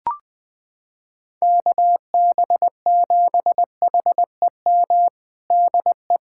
Morse code, heard as a keyed tone, about 20 wpm, 700 Hz. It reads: KB7HEM DE
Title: USA Callsign DE Callsign - 6 Characters - 20wpm